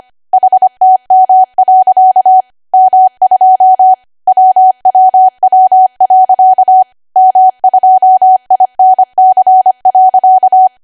Morse code, heard as a keyed tone, about 25 wpm, 750 Hz. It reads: HTM. M2 WWW. M2INC.
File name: Archive / 25 WPM / 25WPM